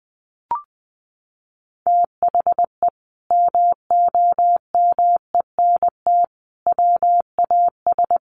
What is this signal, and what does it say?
Morse code, heard as a keyed tone, about 20 wpm, 700 Hz. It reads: THE MOMENT WAS
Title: Sentences from Top 1100 Words 20wpm